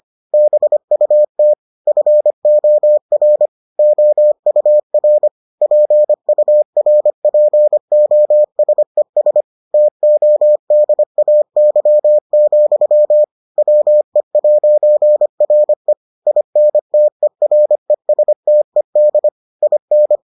Morse code, heard as a keyed tone, about 25 wpm, 600 Hz. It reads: BUT FOR OUR PURPOSES TODAY, WE'RE INTERESTED IN